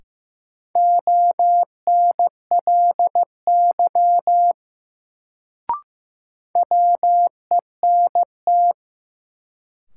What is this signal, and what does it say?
Morse code, heard as a keyed tone, about 15 wpm, 700 Hz. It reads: ONLY WENT
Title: Sets of 2 Words - New Word Drill - 15wpm